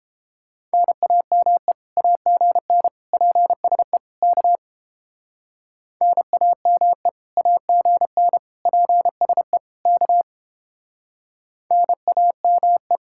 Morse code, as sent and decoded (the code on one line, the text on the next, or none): -. .- -- . / .- --. -. / .--. ... / -.- / -. .- -- . / .- --. -. / .--. ... / -.- / -. .- -- .
NAME AGN PSE K NAME AGN PSE K NAME